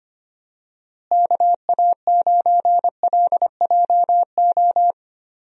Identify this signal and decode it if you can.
Morse code, heard as a keyed tone, about 25 wpm, 700 Hz. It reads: KA9LJO
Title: USA Call Signs - 6 Characters - Repeated 2x - 25wpm